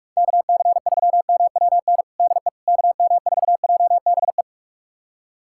Morse code, heard as a keyed tone, about 46 wpm, 700 Hz. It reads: KC3MWN DE KM4JBE
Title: USA Callsign DE Callsign - 6 Characters - 45wpm